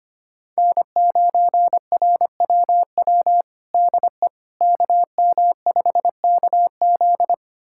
Morse code, heard as a keyed tone, about 25 wpm, 700 Hz. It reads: N9RWW DE KM5KZ
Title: USA Callsign DE Callsign - 5 Characters - 25wpm